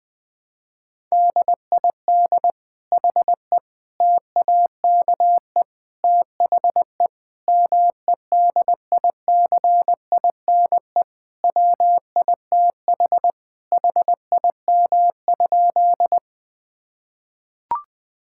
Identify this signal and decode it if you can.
Morse code, heard as a keyed tone, about 21 wpm, 700 Hz. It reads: DID HE TAKE THE MEDICINE WITH HIM?
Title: Sentences from Top 1000 Words - Review Part 6 - 20wpm